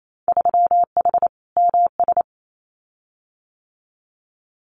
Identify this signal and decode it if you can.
Morse code, heard as a keyed tone, about 28 wpm, 700 Hz. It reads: 3H MS
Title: Winter Field Day Exchanges Encore 28wpm